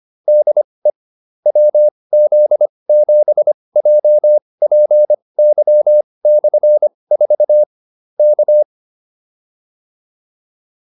Morse code, heard as a keyed tone, about 25 wpm, spaced as for 20 wpm, 600 Hz. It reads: DE WZ7JPY/4 K
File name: QSO309 20